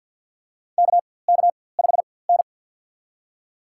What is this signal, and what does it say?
Morse code, heard as a keyed tone, about 52 wpm, spaced as for 23 wpm, 700 Hz. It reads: KK5N
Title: USA Call Signs - 4x Character Spacing - 50wpm